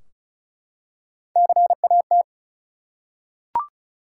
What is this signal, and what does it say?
Morse code, heard as a keyed tone, about 36 wpm, 700 Hz. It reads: CAT
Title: Three Letter Words - 35wpm